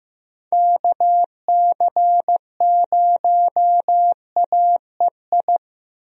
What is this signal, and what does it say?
Morse code, heard as a keyed tone, about 15 wpm, 700 Hz. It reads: KC0AEI
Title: USA Call Signs - 6 Characters - Repeated 2x - 15wpm